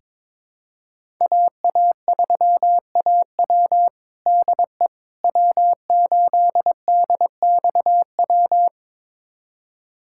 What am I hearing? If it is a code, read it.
Morse code, heard as a keyed tone, about 22 wpm, 700 Hz. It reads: AA3AW DE W8DXW
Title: USA Callsign DE Callsign - 5 Characters - 22wpm